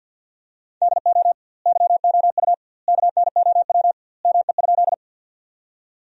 Morse code, heard as a keyed tone, about 52 wpm, 700 Hz. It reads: DO YOU KNOW ME?
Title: 2x Word-Spacing - Sentences from Top 200 Words - 50wpm